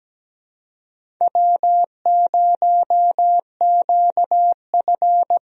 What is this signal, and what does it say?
Morse code, heard as a keyed tone, about 17 wpm, 700 Hz. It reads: W0QF
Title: USA Call Signs - 4 Characters - Repeated 2x - 17wpm